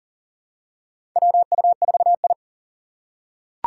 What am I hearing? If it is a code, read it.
Morse code, heard as a keyed tone, about 40 wpm, 700 Hz. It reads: WU4I
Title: USA Call Signs - 4 Characters - 6x Character Spacing - 40wpm